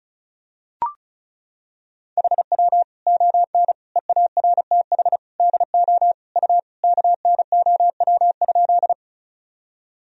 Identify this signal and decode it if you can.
Morse code, heard as a keyed tone, about 36 wpm, 700 Hz. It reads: HW ON EARTH DO U KNOW?
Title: Sentences from Top 200 Words and Abbreviations 35wpm